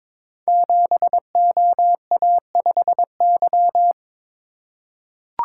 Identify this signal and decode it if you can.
Morse code, heard as a keyed tone, about 22 wpm, 690 Hz. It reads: 7OA5Y
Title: Code Groups - 5 Letter:Numbers - 22wpm